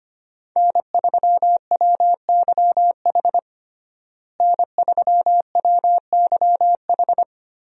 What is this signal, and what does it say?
Morse code, heard as a keyed tone, about 25 wpm, 700 Hz. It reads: N3WYH N3WYH
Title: USA Call Signs - 5 Characters - Repeated 3x - 25wpm